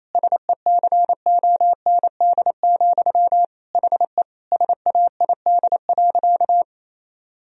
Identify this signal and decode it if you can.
Morse code, heard as a keyed tone, about 28 wpm, 700 Hz. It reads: SECOND, HE SAID.
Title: Sentences from Top 1000 Words - Review Part 1 - 28wpm